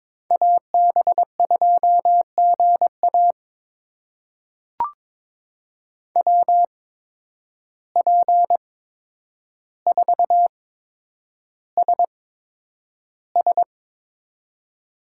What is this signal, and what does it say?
Morse code, heard as a keyed tone, about 22 wpm, spaced as for 10 wpm, 700 Hz. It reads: AB2GA WP4SS
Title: USA Call Signs - 5 Characters - 8x Character Spacing - 22wpm